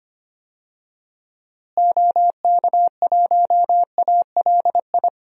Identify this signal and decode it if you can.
Morse code, heard as a keyed tone, about 25 wpm, 700 Hz. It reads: OK1ALI
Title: DXCC-100 Call Signs - 6 Characters - Repeated 3x - 25wpm